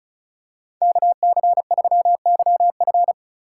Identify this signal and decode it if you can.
Morse code, heard as a keyed tone, about 34 wpm, 700 Hz. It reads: KC3YF
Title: USA Call Signs - 5 Characters - 8x Character Spacing - 35wpm